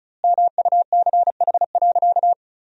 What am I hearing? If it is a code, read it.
Morse code, heard as a keyed tone, about 36 wpm, 700 Hz. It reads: MUCH.
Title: 2x Word-Spacing - Sentences from Top 400 Words - 35wpm